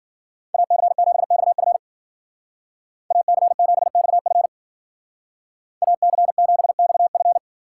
Morse code, heard as a keyed tone, about 52 wpm, 700 Hz. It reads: AC7XF AC7XF AC7XF